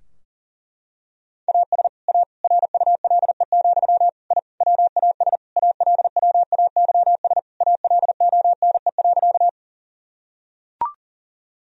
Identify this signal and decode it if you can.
Morse code, heard as a keyed tone, about 40 wpm, 700 Hz. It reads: AS A RULE, I WAS ALWAYS ALONE.
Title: Sentences from Top 1000 Words - Review Part 2 - 40wpm